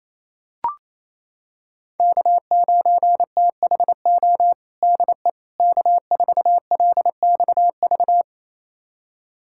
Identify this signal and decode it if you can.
Morse code, heard as a keyed tone, about 28 wpm, 700 Hz. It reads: K9THO DE K4LXV
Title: USA Callsign DE Callsign 28wpm